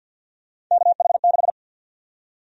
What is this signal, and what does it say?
Morse code, heard as a keyed tone, about 51 wpm, 700 Hz. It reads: KH6
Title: USA Call Sign Prefixes - 50wpm